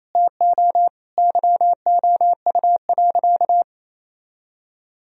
Morse code, heard as a keyed tone, about 28 wpm, 700 Hz. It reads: TO YOU.
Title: Sentences from Top 1000 Words - Review Part 1 - 28wpm